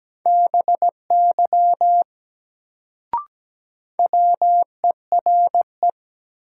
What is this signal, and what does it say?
Morse code, heard as a keyed tone, about 17 wpm, 700 Hz. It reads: BY WERE
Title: Sentences from Top 100 Words - New Word Drill - 17wpm